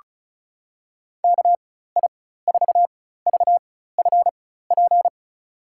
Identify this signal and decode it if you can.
Morse code, heard as a keyed tone, about 36 wpm, spaced as for 16 wpm, 700 Hz. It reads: KI4VFP
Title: USA Call Signs - 6 Characters - 4x Character Spacing - 35wpm